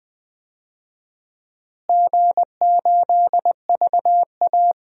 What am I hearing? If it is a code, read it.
Morse code, heard as a keyed tone, about 19 wpm, 700 Hz. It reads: G8VA
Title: DXCC-100 Call Signs - 4 Characters - Repeated 3x - 20wpm